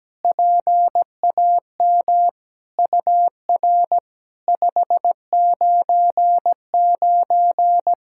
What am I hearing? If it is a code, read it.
Morse code, heard as a keyed tone, about 17 wpm, 700 Hz. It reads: PAM UR 599
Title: SOTA QSO Practice 17wpm